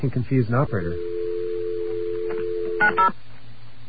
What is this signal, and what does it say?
Dial tone (North America), 350 + 440 Hz, with speech.
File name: mp3files / off the hook